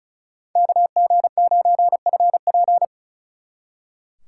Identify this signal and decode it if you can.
Morse code, heard as a keyed tone, about 46 wpm, 700 Hz. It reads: KG9FP